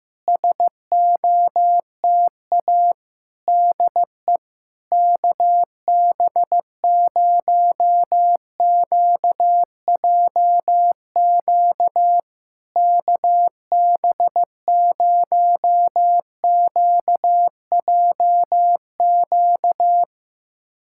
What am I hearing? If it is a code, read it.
Morse code, heard as a keyed tone, about 15 wpm, 690 Hz. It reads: SOTA DE KB0QJQ KB0QJQ
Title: SOTA QSO Practice 15wpm